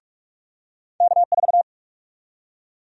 Morse code, heard as a keyed tone, about 46 wpm, 700 Hz. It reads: K4
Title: USA Call Sign Prefixes - 45wpm